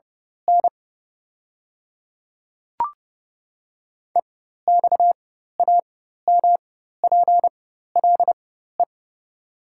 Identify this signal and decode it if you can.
Morse code, heard as a keyed tone, about 32 wpm, spaced as for 14 wpm, 700 Hz. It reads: N EXAMPLE
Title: 4x Character Spacing - Top 500 Words - 30wpm